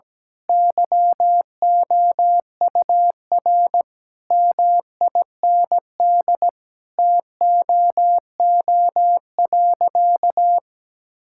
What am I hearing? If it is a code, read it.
Morse code, heard as a keyed tone, about 17 wpm, 700 Hz. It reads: YOUR MIND TOO.
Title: Sentences from Top 1000 Words - Review Part 6 - 17wpm